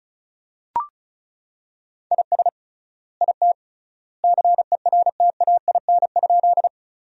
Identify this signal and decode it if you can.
Morse code, heard as a keyed tone, about 34 wpm, 700 Hz. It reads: IS IT CERTAIN?
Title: 3x Word-Spacing - Sentences from Top 500 Words - 35wpm